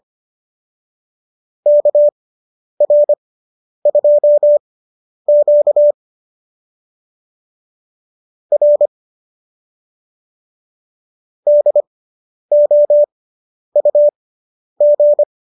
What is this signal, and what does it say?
Morse code, heard as a keyed tone, about 25 wpm, spaced as for 10 wpm, 590 Hz. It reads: KR2Q R DOUG